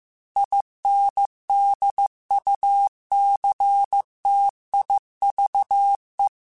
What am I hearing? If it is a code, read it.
Morse code, heard as a keyed tone, about 15 wpm, 800 Hz. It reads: INDUCTIVE
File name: Technical Subjects / Technical Subjects-mp3s / XL @13